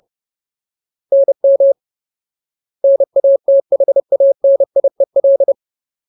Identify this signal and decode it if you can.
Morse code, heard as a keyed tone, about 30 wpm, 550 Hz. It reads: NM NATHANIEL
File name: Ultimate Word List / 30WPM Ultimate